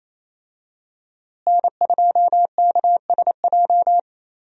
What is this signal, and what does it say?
Morse code, heard as a keyed tone, about 28 wpm, 700 Hz. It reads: N2KSJ
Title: USA Call Signs - 5 Characters - Repeated 3x - 28wpm